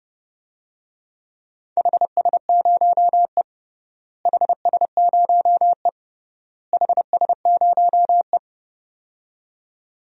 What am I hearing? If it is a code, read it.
Morse code, heard as a keyed tone, about 29 wpm, 700 Hz. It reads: HS0E HS0E HS0E
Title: DXCC-100 Call Signs - 4 Characters - Repeated 3x - 30wpm